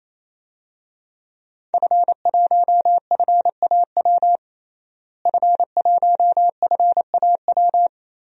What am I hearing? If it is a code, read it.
Morse code, heard as a keyed tone, about 28 wpm, 690 Hz. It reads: F1FAW F1FAW
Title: DXCC-100 Call Signs - 5 Characters - Repeated 3x - 28wpm